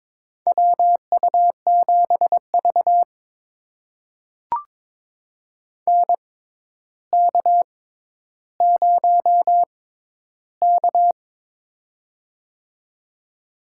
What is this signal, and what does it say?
Morse code, heard as a keyed tone, about 22 wpm, spaced as for 11 wpm, 700 Hz. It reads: WU7V NK0K
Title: USA Call Signs - 6x Character Spacing - 22wpm